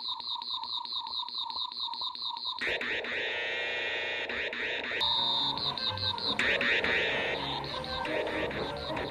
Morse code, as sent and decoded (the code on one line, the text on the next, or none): none